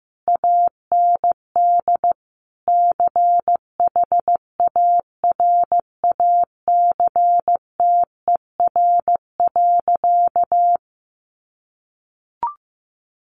Morse code, heard as a keyed tone, about 15 wpm, 700 Hz. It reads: AND CHARACTER.
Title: Sentences from Top 1200 Words 15wpm